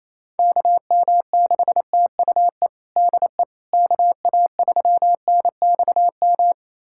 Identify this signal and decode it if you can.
Morse code, heard as a keyed tone, about 27 wpm, 700 Hz. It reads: KM6TUE DE KA3NXM
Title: USA Callsign DE Callsign 28wpm